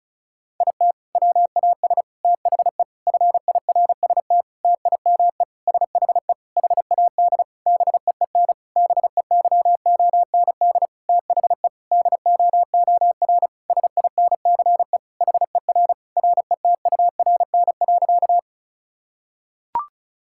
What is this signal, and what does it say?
Morse code, heard as a keyed tone, about 36 wpm, 700 Hz. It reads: IT WAS THE FIRST TIME SHE HAD BEEN BEYOND THE DOOR SINCE HER RETURN.
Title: Sentences from Top 1000 Words - Review Part 6 - 35wpm